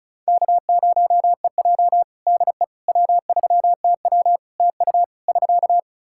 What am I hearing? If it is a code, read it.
Morse code, heard as a keyed tone, about 34 wpm, 700 Hz. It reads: K0EJ DE W3TW TU <SK>
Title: Practice QSO - Exchange 4 - Ending - 35wpm